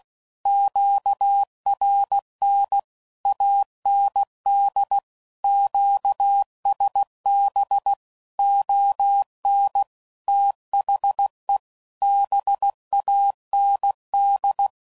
Morse code, heard as a keyed tone, about 16 wpm, 800 Hz. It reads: QRN AND QSB ON THE BAND
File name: CW Code Groups @15WPM / QSOs / WD0BGP Spaced